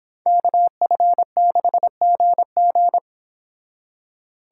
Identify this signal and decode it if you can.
Morse code, heard as a keyed tone, about 26 wpm, 700 Hz. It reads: KF6GG